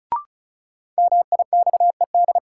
Morse code, heard as a keyed tone, about 34 wpm, 700 Hz. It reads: MIXED
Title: Sentences from Top 1500 Words - New Word Drill - 35wpm